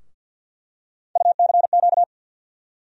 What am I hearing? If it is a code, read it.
Morse code, heard as a keyed tone, about 48 wpm, 700 Hz. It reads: UCQ
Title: USA Call Sign Postfixes - 3 Letters 50wpm